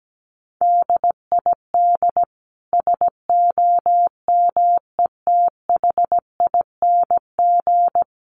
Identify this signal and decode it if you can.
Morse code, heard as a keyed tone, about 17 wpm, 700 Hz. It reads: DID SOMETHING
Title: Sentences from Top 600 Words 17wpm